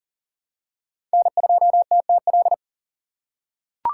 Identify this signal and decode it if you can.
Morse code, heard as a keyed tone, about 40 wpm, 700 Hz. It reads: N2TTL
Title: USA Call Signs - 5 Characters - 3x Character Spacing - 40wpm